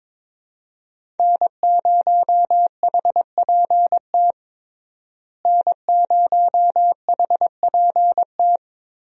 Morse code, heard as a keyed tone, about 22 wpm, 700 Hz. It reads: N0HPT N0HPT
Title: USA Call Signs - 5 Characters - Repeated 3x - 22wpm